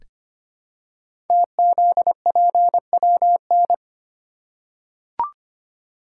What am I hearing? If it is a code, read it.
Morse code, heard as a keyed tone, about 25 wpm, 700 Hz. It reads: TZPWN
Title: Code Groups - 5 Letter:Numbers - 25wpm